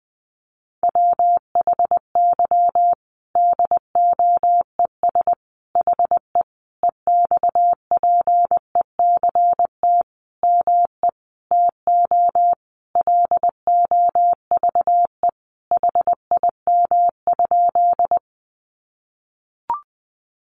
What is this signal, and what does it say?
Morse code, heard as a keyed tone, about 20 wpm, 700 Hz. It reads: WHY DOES HE EXPECT ME TO LOVE HIM?
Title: Sentences from Top 1100 Words 20wpm